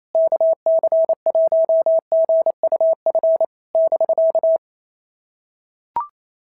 Morse code, heard as a keyed tone, about 28 wpm, 650 Hz. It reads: KC1GUF <BK>
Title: SOTA QSO Practice 28wpm